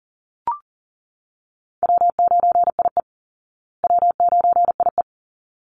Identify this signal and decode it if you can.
Morse code, heard as a keyed tone, about 40 wpm, 700 Hz. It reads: W9IE W9IE
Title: Easy Call Signs Repeated 2x - 40wpm